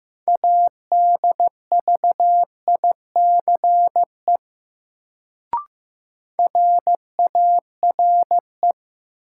Morse code, heard as a keyed tone, about 15 wpm, 700 Hz. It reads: ADVICE RARE
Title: Sentences from Top 1100 Words - New Word Drill - 15wpm